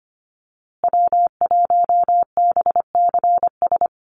Morse code, heard as a keyed tone, about 25 wpm, 700 Hz. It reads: W1BCS
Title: USA Call Signs Repeated 3x 25wpm